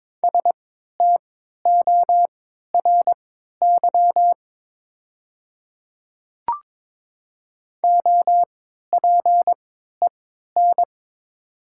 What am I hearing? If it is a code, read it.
Morse code, heard as a keyed tone, about 22 wpm, spaced as for 12 wpm, 700 Hz. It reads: STORY OPEN
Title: 3x Character Spacing - Top 300 Words - 22wpm